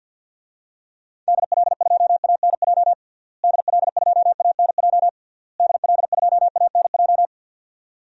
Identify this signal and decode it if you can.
Morse code, heard as a keyed tone, about 50 wpm, 700 Hz. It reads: DL2ANJ DL2ANJ DL2ANJ
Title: DXCC-100 Call Signs - 6 Characters - Repeated 3x - 50wpm